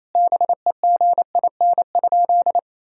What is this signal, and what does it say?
Morse code, heard as a keyed tone, about 28 wpm, 700 Hz. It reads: BEGIN?
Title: Sentences from Top 1000 Words - Review Part 1 - 28wpm